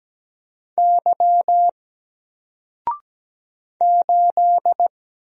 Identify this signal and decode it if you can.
Morse code, heard as a keyed tone, about 17 wpm, 700 Hz. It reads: Y 8
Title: Single Letter-Number Alt 17wpm